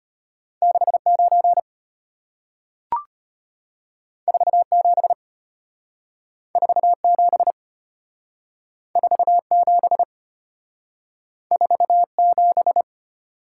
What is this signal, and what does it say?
Morse code, heard as a keyed tone, about 34 wpm, 700 Hz. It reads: 69 47 47 47 47